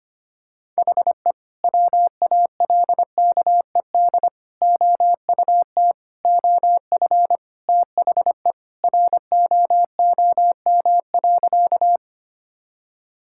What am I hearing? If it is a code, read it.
Morse code, heard as a keyed tone, about 25 wpm, 700 Hz. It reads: HE WALKED OUT OF THE ROOM.
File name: practice / 25wpm / Sentences with 2-Word Phrases - 25wpm